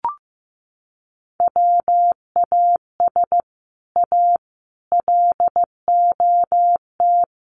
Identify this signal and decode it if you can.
Morse code, heard as a keyed tone, about 15 wpm, 700 Hz. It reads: WAS A LOT
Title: Sets of 4 Words 15wpm